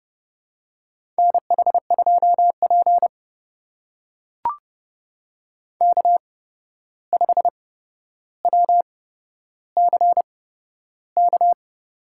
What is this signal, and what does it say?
Morse code, heard as a keyed tone, about 30 wpm, spaced as for 12 wpm, 700 Hz. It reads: NH2P K5WCK
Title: USA Call Signs - 8x Character Spacing - 30wpm